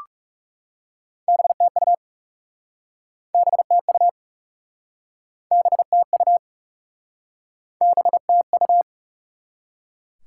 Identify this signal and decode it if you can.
Morse code, heard as a keyed tone, about 40 wpm, 700 Hz. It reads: BTU BTU BTU BTU